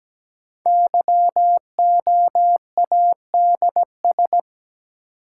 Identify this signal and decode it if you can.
Morse code, heard as a keyed tone, about 17 wpm, 700 Hz. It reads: YOADS
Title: Code Groups - 5 Letter:Numbers - 17wpm